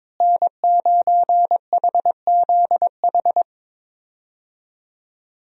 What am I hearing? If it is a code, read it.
Morse code, heard as a keyed tone, about 22 wpm, 700 Hz. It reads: N9HZH
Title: USA Call Signs - 5 Characters - Repeated 3x - 22wpm